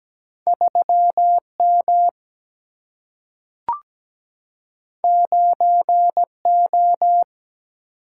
Morse code, heard as a keyed tone, about 17 wpm, 700 Hz. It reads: 3 M 9 O
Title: Single-Letter Single-Number 17wpm